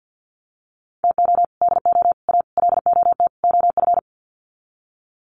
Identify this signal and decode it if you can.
Morse code, heard as a keyed tone, about 50 wpm, 700 Hz. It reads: TO DO A LOT OF
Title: Sets of 5 Words Encore 50wpm